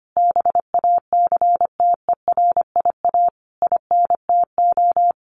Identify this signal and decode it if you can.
Morse code, heard as a keyed tone, about 25 wpm, 700 Hz. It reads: BACTERIA INTO
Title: A Sound Of Thunder 25wpm